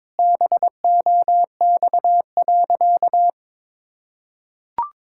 Morse code, heard as a keyed tone, about 22 wpm, 700 Hz. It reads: BOX.
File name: practice / 22wpm / Sentences with 4-Word Phrases - 22wpm